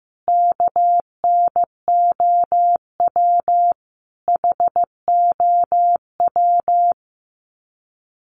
Morse code, heard as a keyed tone, about 15 wpm, 700 Hz. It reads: KNOW HOW
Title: Sets of 5 Words 15wpm